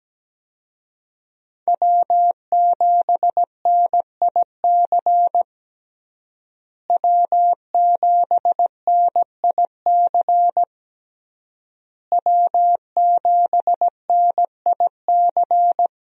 Morse code, heard as a keyed tone, about 17 wpm, 700 Hz. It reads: W7NIC W7NIC W7NIC
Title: USA Call Signs - 5 Characters - Repeated 3x - 17wpm